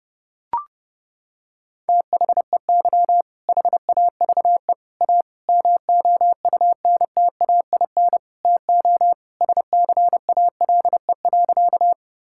Morse code, heard as a keyed tone, about 30 wpm, 700 Hz. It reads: THEY HAVE A MOUNTAIN TO SCALE.
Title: Sentences from Top 1000 Words - Review Part 5 - 30wpm